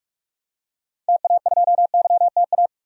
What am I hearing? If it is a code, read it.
Morse code, heard as a keyed tone, about 43 wpm, 700 Hz. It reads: TA2YTA
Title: DXCC-100 Call Signs - 6 Characters - 45wpm